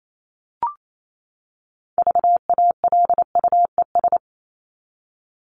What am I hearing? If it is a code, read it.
Morse code, heard as a keyed tone, about 28 wpm, 700 Hz. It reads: VALUES
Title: ES:S Suffix - Top 100 Words with Base Words 28wpm